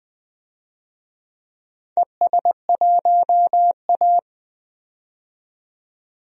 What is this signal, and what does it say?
Morse code, heard as a keyed tone, about 20 wpm, 700 Hz. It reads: ES1A
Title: DXCC-100 Call Signs - 4 Characters - 20wpm